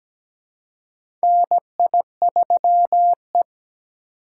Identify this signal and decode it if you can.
Morse code, heard as a keyed tone, about 17 wpm, 700 Hz. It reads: NI3E
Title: USA Call Signs - 4 Characters - 2x Character Spacing - 17wpm